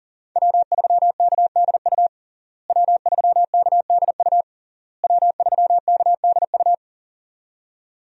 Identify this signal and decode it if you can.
Morse code, heard as a keyed tone, about 40 wpm, 700 Hz. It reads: W3KDU W3KDU W3KDU
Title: USA Call Signs - 5 Characters - Repeated 3x - 40wpm